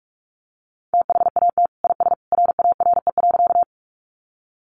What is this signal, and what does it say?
Morse code, heard as a keyed tone, about 44 wpm, 700 Hz. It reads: THAT IS RARE.
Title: Sentences from Top 1100 Words 45wpm